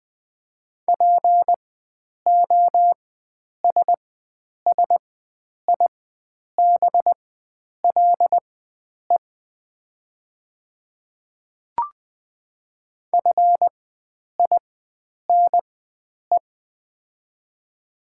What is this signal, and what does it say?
Morse code, heard as a keyed tone, about 20 wpm, spaced as for 9 wpm, 700 Hz. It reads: POSSIBLE FINE
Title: 4x Character Spacing - Top 500 Words - 20wpm